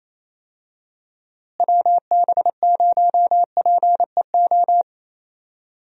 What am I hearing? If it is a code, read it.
Morse code, heard as a keyed tone, about 28 wpm, 700 Hz. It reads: WB0PEO
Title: USA Call Signs - 6 Characters - Repeated 2x - 28wpm